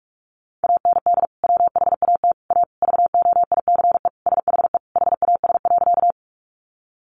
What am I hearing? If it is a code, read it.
Morse code, heard as a keyed tone, about 44 wpm, 700 Hz. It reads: AND WHAT A VOICE SHE HAS.
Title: Sentences from Top 500 Words 45wpm